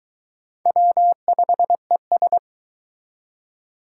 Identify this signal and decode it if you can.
Morse code, heard as a keyed tone, about 23 wpm, 700 Hz. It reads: W5ES